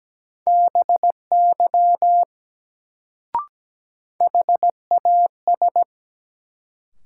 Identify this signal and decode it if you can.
Morse code, heard as a keyed tone, about 17 wpm, 700 Hz. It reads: BY HAS
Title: Sets of 2 Words - New Word Drill - 17wpm